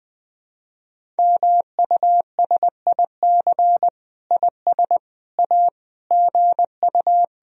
Morse code, heard as a keyed tone, about 21 wpm, 700 Hz. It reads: MUSIC IS A GU
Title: Sentences from Top 300 Words and Abbreviations 20wpm